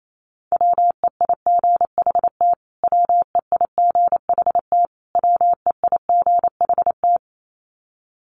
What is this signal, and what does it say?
Morse code, heard as a keyed tone, about 28 wpm, 700 Hz. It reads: WEIGHT WEIGHT WEIGHT
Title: Top 500 Words Repeated 28wpm